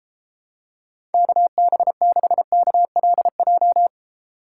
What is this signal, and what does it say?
Morse code, heard as a keyed tone, about 32 wpm, 700 Hz. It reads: KB6KLJ